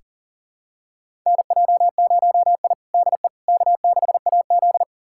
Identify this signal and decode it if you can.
Morse code, heard as a keyed tone, about 40 wpm, 700 Hz. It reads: NJ0I DE K6AZ
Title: USA Callsign DE Callsign - 4 Characters - 40wpm